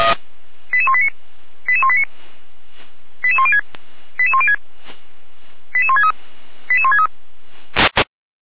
Five-tone selective call (ZVEI), heard as quick runs of five tones, tones of about 80 ms. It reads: 88189 88189 88106 88106 88163 88163
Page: SDR/ZVEI